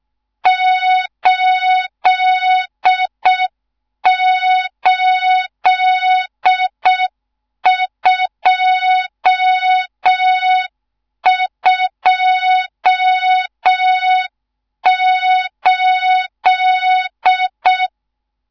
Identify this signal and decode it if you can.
Morse code, heard as a keyed tone, about 6 wpm, 3700 Hz. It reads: 88228
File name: Morze-HA2MN / hang / 14 280 HA2MN